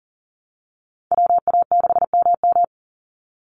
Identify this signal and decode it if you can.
Morse code, heard as a keyed tone, about 40 wpm, 700 Hz. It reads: WA6MM
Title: NA SOTA Chaser Callsigns 40wpm